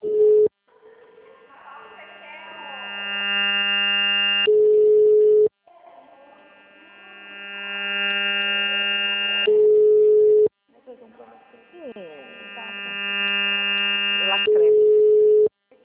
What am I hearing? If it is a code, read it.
Ringback tone (Europe, 425 Hz), with speech.
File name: out- -2002- - - .50